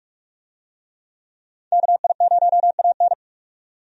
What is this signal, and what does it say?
Morse code, heard as a keyed tone, about 46 wpm, 700 Hz. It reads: KI0AN
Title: USA Call Signs - 5 Characters - 45wpm